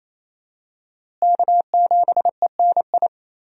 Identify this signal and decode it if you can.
Morse code, heard as a keyed tone, about 28 wpm, 700 Hz. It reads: K7ENI
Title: USA Call Signs - 5 Characters - Repeated 2x - 28wpm